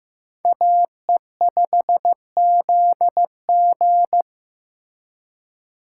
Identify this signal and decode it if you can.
Morse code, heard as a keyed tone, about 18 wpm, 700 Hz. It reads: AE5ZG
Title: Speed Racing - USA 2x2 Call Signs 15wpm